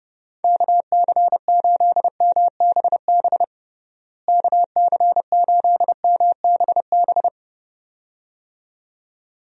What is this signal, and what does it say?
Morse code, heard as a keyed tone, about 30 wpm, 700 Hz. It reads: KC8MBB KC8MBB
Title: USA Call Signs - 6 Characters - Repeated 2x - 30wpm